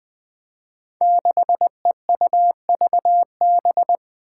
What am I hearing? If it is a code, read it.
Morse code, heard as a keyed tone, about 20 wpm, 700 Hz. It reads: 6EUVB
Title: Five Challenging Letter-Number Combinations 20wpm